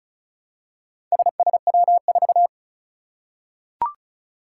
Morse code, heard as a keyed tone, about 34 wpm, 700 Hz. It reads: SSW4
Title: Four Challenging Letter-Number Combinations 35wpm